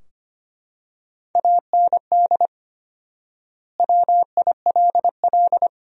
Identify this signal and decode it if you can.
Morse code, heard as a keyed tone, about 25 wpm, 700 Hz. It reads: AND WILL
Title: 4x Word-Spacing - Sentences from Top 400 Words - 25wpm